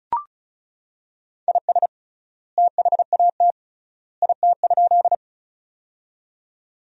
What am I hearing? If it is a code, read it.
Morse code, heard as a keyed tone, about 34 wpm, 700 Hz. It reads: IS THAT IT?
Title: 3x Word-Spacing - Sentences from Top 100 Words - 35wpm